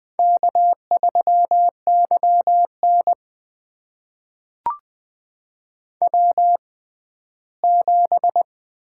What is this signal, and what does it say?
Morse code, heard as a keyed tone, about 20 wpm, spaced as for 14 wpm, 700 Hz. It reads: K3YN W7
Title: USA Call Signs - 4 Characters - 6x Character Spacing - 20wpm